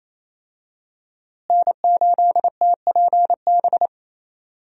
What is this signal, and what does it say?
Morse code, heard as a keyed tone, about 28 wpm, 700 Hz. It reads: N8TPB
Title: USA Call Signs - 5 Characters - Repeated 3x - 28wpm